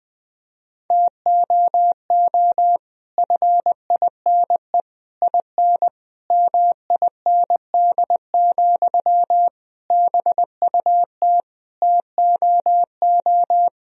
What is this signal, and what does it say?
Morse code, heard as a keyed tone, about 20 wpm, 700 Hz. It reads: TOO FINE IN MIND, BUT TOO
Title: Sentences from Top 1000 Words - Review Part 4 - 20wpm